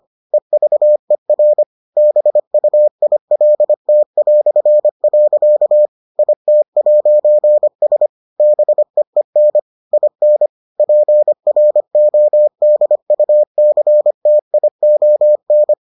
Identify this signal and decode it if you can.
Morse code, heard as a keyed tone, about 25 wpm, 600 Hz. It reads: EVER BUILT". IT'S BEEN IN PRODUCTION